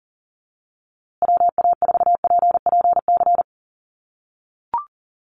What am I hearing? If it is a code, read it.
Morse code, heard as a keyed tone, about 40 wpm, 700 Hz. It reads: WA4PPC
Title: USA Call Signs Repeated 2x 40wpm